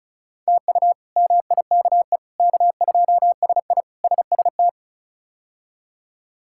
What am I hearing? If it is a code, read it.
Morse code, heard as a keyed tone, about 34 wpm, 700 Hz. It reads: TU MIKE K2SI SST
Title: K1USN SST QSO Practice - 35wpm